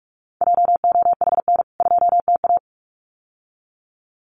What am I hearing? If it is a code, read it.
Morse code, heard as a keyed tone, about 44 wpm, 700 Hz. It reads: JOHN 2TA
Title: CWT Exchange 45wpm